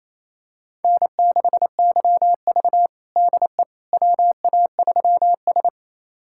Morse code, heard as a keyed tone, about 28 wpm, 700 Hz. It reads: N6YV DE WA3S
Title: USA Callsign DE Callsign - 4 Characters - 28wpm